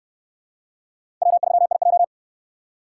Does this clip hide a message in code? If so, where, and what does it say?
Yes: Morse code from 1.2 to 2.0 s, heard as a keyed tone, about 68 wpm, 700 Hz. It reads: W3IJ